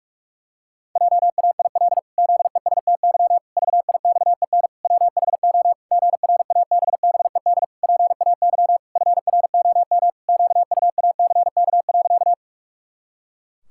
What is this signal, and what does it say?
Morse code, heard as a keyed tone, about 46 wpm, 700 Hz. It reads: JAIL ZESTY VIXEN WHO GRABBED PAY FROM QUACK.
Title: Pangram Challenge v2 - 45wpm